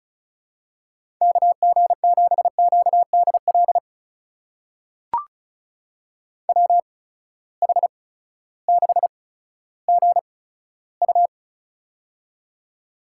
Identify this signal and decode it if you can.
Morse code, heard as a keyed tone, about 36 wpm, spaced as for 16 wpm, 700 Hz. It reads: KG7QDL WH6GU
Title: USA Call Signs - 8x Character Spacing - 35wpm